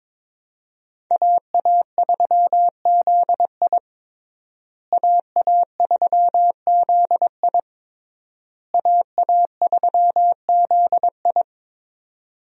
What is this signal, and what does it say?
Morse code, heard as a keyed tone, about 22 wpm, 700 Hz. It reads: AA3ZI AA3ZI AA3ZI
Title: USA Call Signs - 5 Characters - Repeated 3x - 22wpm